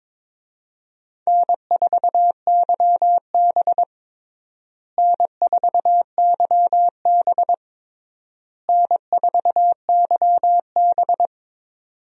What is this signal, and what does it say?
Morse code, heard as a keyed tone, about 22 wpm, 700 Hz. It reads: N4YB N4YB N4YB
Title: USA Call Signs - 4 Characters - Repeated 3x - 22wpm